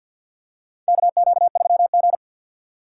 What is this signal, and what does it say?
Morse code, heard as a keyed tone, about 50 wpm, 700 Hz. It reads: KQ3G
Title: USA Call Signs - 4 Characters - 3x Character Spacing - 50wpm